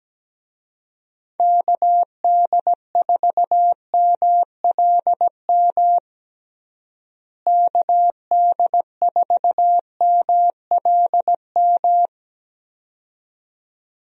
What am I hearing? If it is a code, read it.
Morse code, heard as a keyed tone, about 17 wpm, 700 Hz. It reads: KD4MLM KD4MLM
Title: USA Call Signs - 6 Characters - Repeated 2x - 17wpm